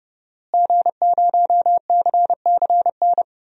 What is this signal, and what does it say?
Morse code, heard as a keyed tone, about 30 wpm, 700 Hz. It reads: G0CCN
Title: DXCC-100 Call Signs - 5 Characters - Repeated 3x - 30wpm